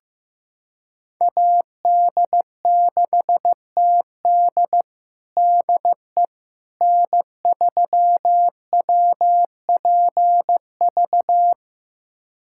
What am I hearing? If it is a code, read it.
Morse code, heard as a keyed tone, about 15 wpm, 700 Hz. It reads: AD6TD DE N3WPV
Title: USA Callsign DE Callsign - 5 Characters - 15wpm